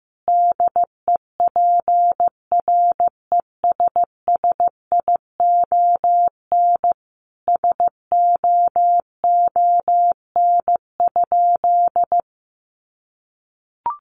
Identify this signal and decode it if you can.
Morse code, heard as a keyed tone, about 14 wpm, 700 Hz. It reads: DEPRESSION SOON?
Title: Sentences from Top 1300 Words 15wpm